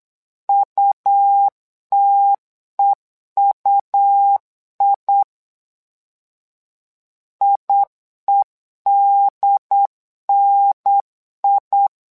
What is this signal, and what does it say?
Morse code, heard as a keyed tone, about 8 wpm, 800 Hz. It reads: UTEUI IEDNI
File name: LEZIONI / 2 UHDN / 2 UHDN 8wpm2232